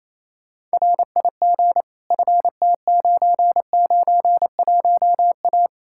Morse code, heard as a keyed tone, about 37 wpm, 700 Hz. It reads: RIG FT991A